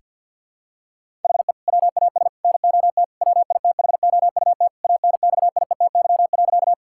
Morse code, heard as a keyed tone, about 51 wpm, 700 Hz. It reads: HE WAS NOT WITHOUT ANXIETY.
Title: Sentences from Top 1600 Words 50wpm